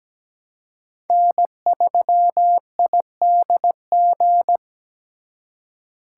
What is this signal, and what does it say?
Morse code, heard as a keyed tone, about 20 wpm, 700 Hz. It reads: N3IDG
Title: Speed Racing - USA 1x3 Call Signs 17wpm